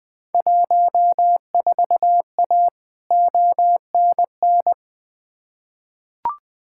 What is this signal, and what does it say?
Morse code, heard as a keyed tone, about 21 wpm, 700 Hz. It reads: 14A ONN
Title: Field Day Exchanges Encore - 20wpm